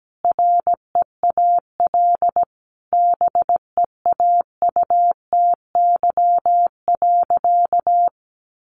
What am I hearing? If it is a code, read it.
Morse code, heard as a keyed tone, about 17 wpm, 700 Hz. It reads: REAL BEAUTY.
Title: Sentences from Top 1100 Words 17wpm